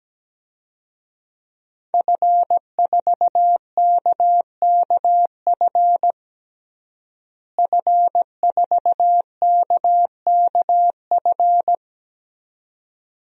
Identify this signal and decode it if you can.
Morse code, heard as a keyed tone, about 17 wpm, 700 Hz. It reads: F4KKF F4KKF
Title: DXCC-100 Call Signs - 5 Characters - Repeated 2x - 17wpm